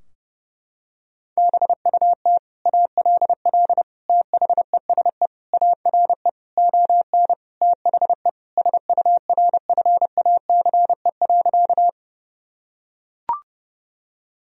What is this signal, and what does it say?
Morse code, heard as a keyed tone, about 32 wpm, 700 Hz. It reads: BUT ALL THESE ARE ON THE SURFACE.
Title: Sentences from Top 1000 Words - Review Part 3 - 30wpm